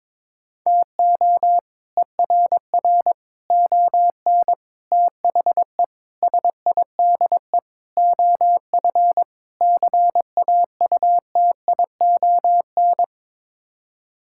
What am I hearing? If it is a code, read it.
Morse code, heard as a keyed tone, about 22 wpm, 700 Hz. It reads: TO ERR ON THE SIDE OF CAUTION
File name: practice / 22wpm / Sets of 7 Words - 22wpm